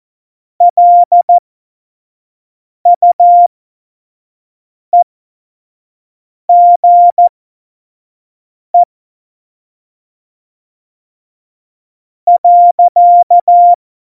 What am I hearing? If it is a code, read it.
Morse code, heard as a keyed tone, about 14 wpm, spaced as for 4 wpm, 700 Hz. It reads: LUEGE.
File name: Morsekurs / Witze / Witz 27 March 2018, 25 F70